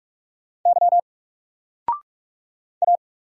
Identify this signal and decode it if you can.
Morse code, heard as a keyed tone, about 44 wpm, 700 Hz. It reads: Y A
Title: Single Letter-Number 45wpm